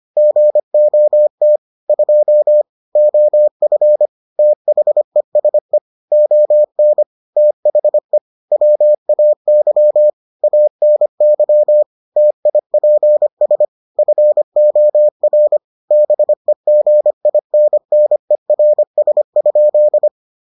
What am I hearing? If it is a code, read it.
Morse code, heard as a keyed tone, about 25 wpm, 600 Hz. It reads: GOT 2 OF THESE ON THE WAY ANY TIPS FOR BEGINNERS?